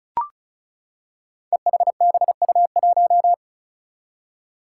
Morse code, heard as a keyed tone, about 36 wpm, 700 Hz. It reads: EHBU1
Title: Five Challenging Letter-Number Combinations 35wpm